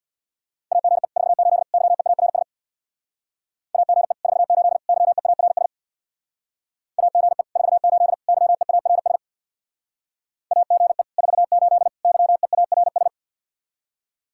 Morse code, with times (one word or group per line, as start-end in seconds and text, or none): 0.7-1.1 AGE
1.2-1.6 48
1.7-2.4 YEARS
3.7-4.1 AGE
4.2-4.8 48
4.9-5.7 YEARS
7.0-7.4 AGE
7.6-8.1 48
8.3-9.2 YEARS
10.5-11.0 AGE
11.2-11.9 48
12.0-13.1 YEARS